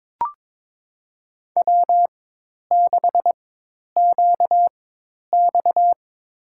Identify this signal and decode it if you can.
Morse code, heard as a keyed tone, about 22 wpm, spaced as for 10 wpm, 700 Hz. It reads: W6QX
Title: USA Call Signs - 4 Characters - 4x Character Spacing - 22wpm